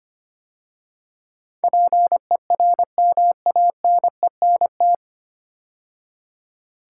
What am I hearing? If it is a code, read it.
Morse code, heard as a keyed tone, about 25 wpm, 700 Hz. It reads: PERMANENT
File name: practice / 25wpm / 9 Letter Words - First 3 Letters Given - 25wpm